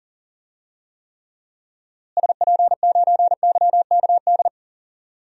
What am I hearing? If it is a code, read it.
Morse code, heard as a keyed tone, about 40 wpm, 700 Hz. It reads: SP9YKD
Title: DXCC-100 Call Signs - 6 Characters - Repeated 2x - 40wpm